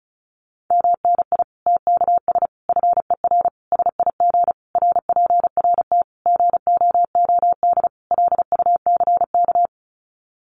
Morse code, heard as a keyed tone, about 34 wpm, 700 Hz. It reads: MNI TKS FER SIG RPRT GOOD LUCK
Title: Common QSO Phrases 35wpm